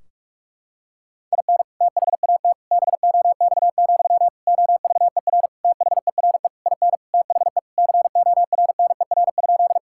Morse code, heard as a keyed tone, about 42 wpm, 700 Hz. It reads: IN THAT BOX, OVER THERE IN THE CORNER?
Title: Sentences from Top 1600 Words 45wpm